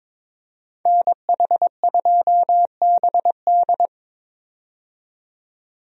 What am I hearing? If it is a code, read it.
Morse code, heard as a keyed tone, about 21 wpm, 700 Hz. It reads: NH2BD
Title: USA Call Signs - 5 Characters - 22wpm